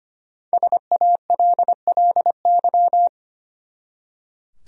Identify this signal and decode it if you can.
Morse code, heard as a keyed tone, about 25 wpm, 700 Hz. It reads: SALLY
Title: USA Common Names - 25wpm